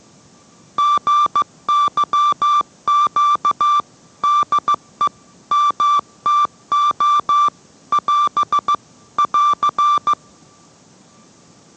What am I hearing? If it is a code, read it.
Morse code, heard as a keyed tone, about 15 wpm, 1170 Hz. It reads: GYQ DE MTO <AS> <AR>